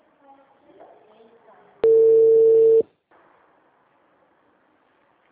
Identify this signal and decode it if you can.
Ringback tone (China, 450 Hz), with speech.